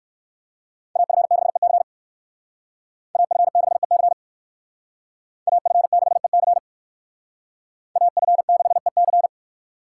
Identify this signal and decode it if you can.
Morse code, heard as a keyed tone, about 60 wpm, 700 Hz. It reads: AF6EC AF6EC AF6EC AF6EC